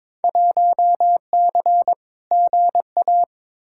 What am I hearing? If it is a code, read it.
Morse code, heard as a keyed tone, about 22 wpm, 700 Hz. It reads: 1C GA
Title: Field Day Exchanges Encore - 22wpm